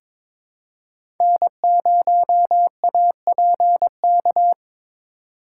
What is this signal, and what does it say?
Morse code, heard as a keyed tone, about 22 wpm, 700 Hz. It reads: N0APK
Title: USA Call Signs - 4x Character Spacing - 22wpm